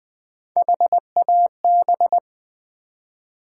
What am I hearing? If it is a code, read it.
Morse code, heard as a keyed tone, about 20 wpm, 700 Hz. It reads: HAB
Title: Build A Word - 1 Letter At A Time - 5-Letter Words - 20wpm